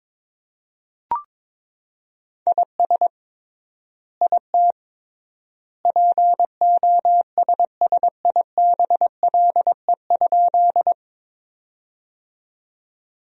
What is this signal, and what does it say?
Morse code, heard as a keyed tone, about 22 wpm, 700 Hz. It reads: IS IT POSSIBLE?
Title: 3x Word-Spacing - Sentences from Top 500 Words - 22wpm